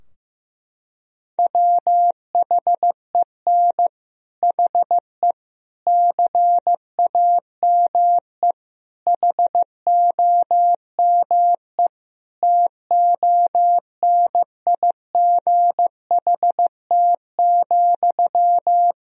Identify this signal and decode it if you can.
Morse code, heard as a keyed tone, about 16 wpm, 700 Hz. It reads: WHEN HE CAME HOME TONIGHT,
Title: Sentences from Top 1000 Words - Review Part 3 - 15wpm